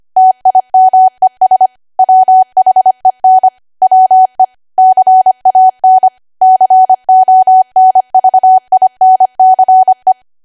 Morse code, heard as a keyed tone, about 25 wpm, 750 Hz. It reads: TIMES WHEN WE CAN CONVINCE
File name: W1AW-Level3/140218 25WPM